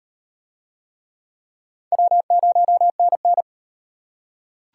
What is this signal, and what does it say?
Morse code, heard as a keyed tone, about 37 wpm, 700 Hz. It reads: W0NN